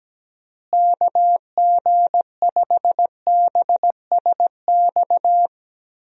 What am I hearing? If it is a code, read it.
Morse code, heard as a keyed tone, about 17 wpm, 700 Hz. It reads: KG5BSX
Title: USA Call Signs - 6 Characters - Repeated 3x - 17wpm